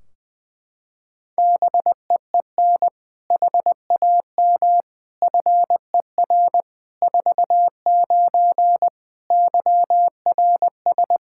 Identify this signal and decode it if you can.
Morse code, heard as a keyed tone, about 20 wpm, 700 Hz. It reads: BEEN HAM FER 49 YRS
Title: Practice QSO - Exchange 3 - More Info - 20wpm